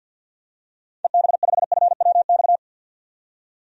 Morse code, heard as a keyed tone, about 51 wpm, 700 Hz. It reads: EB5FWX
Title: DXCC-100 Call Signs - 6 Characters - 50wpm